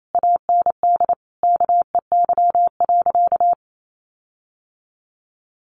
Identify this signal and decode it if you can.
Morse code, heard as a keyed tone, about 28 wpm, 700 Hz. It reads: AND KEY.
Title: Sentences from Top 1400 Words 28wpm